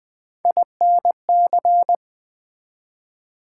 Morse code, heard as a keyed tone, about 20 wpm, 700 Hz. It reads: INC
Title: Build A Word - 3 Letters At A Time - 9-Letter Words - 20wpm